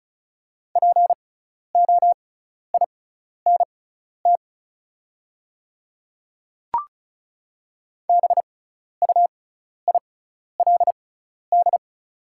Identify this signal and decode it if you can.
Morse code, heard as a keyed tone, about 34 wpm, spaced as for 12 wpm, 700 Hz. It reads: POINT BUILD
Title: 6x Character Spacing - Top 200 Words - 35wpm